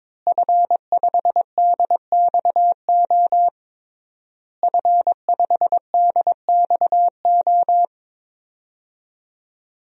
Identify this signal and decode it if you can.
Morse code, heard as a keyed tone, about 22 wpm, 700 Hz. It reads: F5DXO F5DXO
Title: DXCC-100 Call Signs - 5 Characters - Repeated 3x - 22wpm